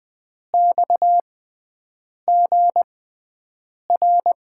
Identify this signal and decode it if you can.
Morse code, heard as a keyed tone, about 20 wpm, spaced as for 7 wpm, 700 Hz. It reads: XGR
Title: USA Call Signs - 6x Character Spacing - 20wpm